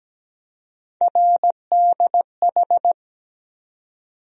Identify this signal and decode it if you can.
Morse code, heard as a keyed tone, about 17 wpm, 700 Hz. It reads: RDH
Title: USA Call Sign Postfixes - 3 Letters 17wpm